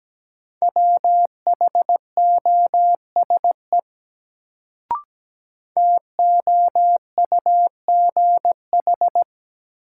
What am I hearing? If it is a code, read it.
Morse code, heard as a keyed tone, about 17 wpm, 700 Hz. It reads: WHOSE TOUGH
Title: Sentences from Top 1200 Words - New Word Drill Alt - 17wpm